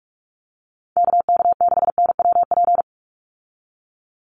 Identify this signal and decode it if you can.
Morse code, heard as a keyed tone, about 44 wpm, 700 Hz. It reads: KK6NWP
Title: USA Call Signs 45wpm